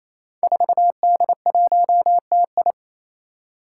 Morse code, heard as a keyed tone, about 27 wpm, 700 Hz. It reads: 4D1TI
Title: Five Challenging Letter-Number Combinations 28wpm